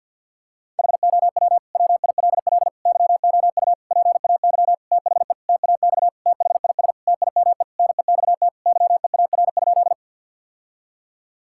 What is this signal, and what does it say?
Morse code, heard as a keyed tone, about 50 wpm, 700 Hz. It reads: HOW WILL YOU PAY THE TAX THIS TIME NEXT YEAR?
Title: Sentences from Top 1000 Words - Review Part 5 - 50wpm